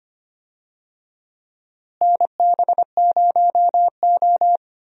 Morse code, heard as a keyed tone, about 25 wpm, 700 Hz. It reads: NB0O
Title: USA Call Signs - 4 Characters - Repeated 2x - 25wpm